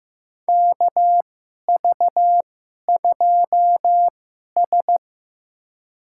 Morse code, heard as a keyed tone, about 15 wpm, spaced as for 11 wpm, 700 Hz. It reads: KV2S
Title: USA Call Signs - 4 Characters - 2x Character Spacing - 15wpm